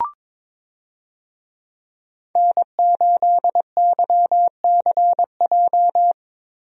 Morse code, heard as a keyed tone, about 22 wpm, 700 Hz. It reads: N8YCJ
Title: USA Call Signs - 5 Characters - Repeated 3x - 22wpm